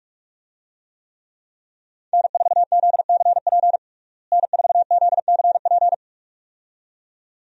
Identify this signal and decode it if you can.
Morse code, heard as a keyed tone, about 42 wpm, 700 Hz. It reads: N4ZCP N4ZCP
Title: USA Call Signs - 5 Characters - Repeated 2x - 45wpm